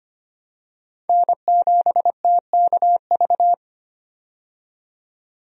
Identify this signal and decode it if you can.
Morse code, heard as a keyed tone, about 25 wpm, 700 Hz. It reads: N7TKV
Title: USA Call Signs - 5 Characters - 25wpm